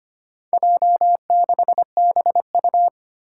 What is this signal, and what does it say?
Morse code, heard as a keyed tone, about 25 wpm, 700 Hz. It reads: J6BU
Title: Four Challenging Letter-Number Combinations 25wpm